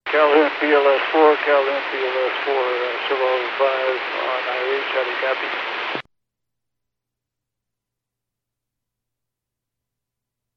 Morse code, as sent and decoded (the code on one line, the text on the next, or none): none